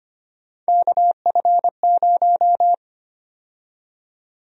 Morse code, heard as a keyed tone, about 25 wpm, 700 Hz. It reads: KF0
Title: USA Call Sign Prefixes - 25wpm